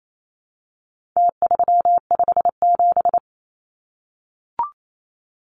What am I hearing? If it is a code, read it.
Morse code, heard as a keyed tone, about 28 wpm, 700 Hz. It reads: T357
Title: Numbers - Contest Style - 4-Digit 28wpm